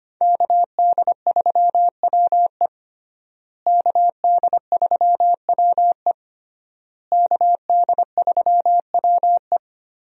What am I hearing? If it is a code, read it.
Morse code, heard as a keyed tone, about 25 wpm, 700 Hz. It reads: KD3WE KD3WE KD3WE
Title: USA Call Signs - 5 Characters - Repeated 3x - 25wpm